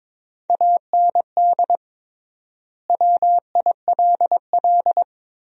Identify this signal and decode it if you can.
Morse code, heard as a keyed tone, about 22 wpm, 700 Hz. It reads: AND WILL
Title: 3x Word-Spacing - Sentences from Top 400 Words - 22wpm